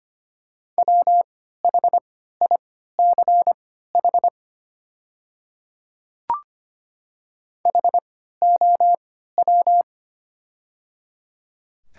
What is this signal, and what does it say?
Morse code, heard as a keyed tone, about 25 wpm, spaced as for 14 wpm, 690 Hz. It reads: WHICH HOW
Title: 3x Character Spacing - Top 200 Words - 25wpm